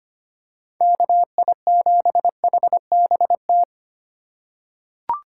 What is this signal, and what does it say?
Morse code, heard as a keyed tone, about 25 wpm, 700 Hz. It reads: KI7HBT
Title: USA Call Signs - 6 Characters - 3x Character Spacing - 25wpm